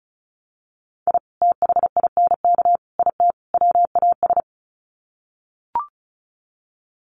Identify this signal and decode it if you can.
Morse code, heard as a keyed tone, about 34 wpm, 700 Hz. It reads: I THINK IT WAS
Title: Sets of 4 Words 35wpm